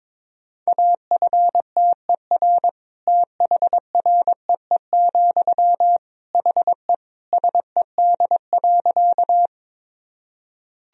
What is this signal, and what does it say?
Morse code, heard as a keyed tone, about 22 wpm, 700 Hz. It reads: AFTER THREE, HE SED.
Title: Sentences from Top 200 Words and Abbreviations 22wpm